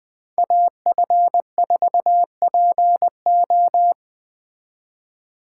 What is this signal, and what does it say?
Morse code, heard as a keyed tone, about 20 wpm, 700 Hz. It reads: AF4PO
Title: Speed Racing - USA Call Signs 17wpm